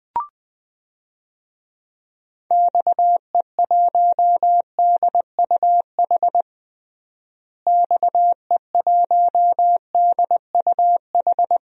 Morse code, heard as a keyed tone, about 20 wpm, 700 Hz. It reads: XE1DUH XE1DUH
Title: DXCC-100 Call Signs - 6 Characters - Repeated 2x - 20wpm